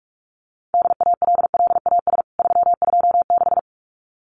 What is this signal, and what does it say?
Morse code, heard as a keyed tone, about 44 wpm, 700 Hz. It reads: DALLAS 326
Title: CWT Exchange 45wpm